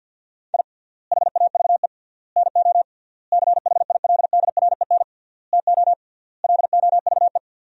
Morse code, heard as a keyed tone, about 50 wpm, 700 Hz. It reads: I HAVE NO CHILDREN TO LOVE
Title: 3x Word-Spacing - Sentences from Top 400 Words - 50wpm